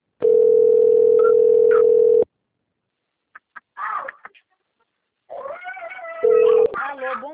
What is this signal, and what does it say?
Ringback tone (North America), 440 + 480 Hz, with speech.
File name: rg-1600-unknown- - - .646